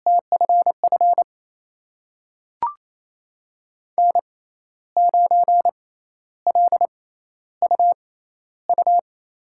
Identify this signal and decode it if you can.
Morse code, heard as a keyed tone, about 28 wpm, spaced as for 12 wpm, 700 Hz. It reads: TFF N9LUU
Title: USA Call Signs - 5 Characters - 6x Character Spacing - 28wpm